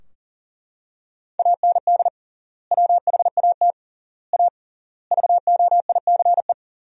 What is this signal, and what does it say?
Morse code, heard as a keyed tone, about 40 wpm, 700 Hz. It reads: AND WHAT A VOICE
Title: 3x Word-Spacing - Sentences from Top 500 Words - 40wpm